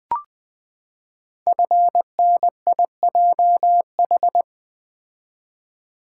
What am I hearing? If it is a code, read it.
Morse code, heard as a keyed tone, about 20 wpm, 700 Hz. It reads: FNIJH
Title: Code Groups - 5 Letters - 20wpm